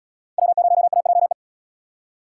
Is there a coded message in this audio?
Yes: Morse code from 0.4 to 1.3 s, heard as a keyed tone, about 77 wpm, 700 Hz. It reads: W0AJE